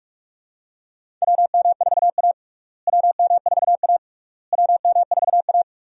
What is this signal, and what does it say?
Morse code, heard as a keyed tone, about 42 wpm, 700 Hz. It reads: WM4A WM4A WM4A
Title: USA Call Signs - 4 Characters - Repeated 3x - 45wpm